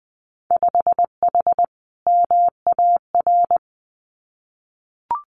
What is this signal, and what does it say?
Morse code, heard as a keyed tone, about 20 wpm, 700 Hz. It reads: 5H MAR
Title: Winter Field Day Exchanges Encore 20wpm